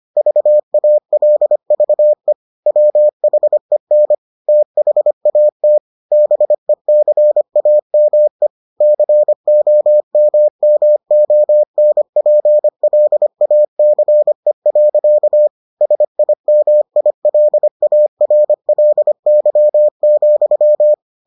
Morse code, heard as a keyed tone, about 25 wpm, 600 Hz. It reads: VALVE WHEN THAT BECAME COMMONPLACE. SIMILARLY,